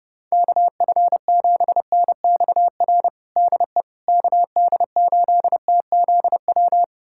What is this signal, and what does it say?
Morse code, heard as a keyed tone, about 32 wpm, 700 Hz. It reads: KF7NXR DE KD8TZW
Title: USA Callsign DE Callsign - 6 Characters - 30wpm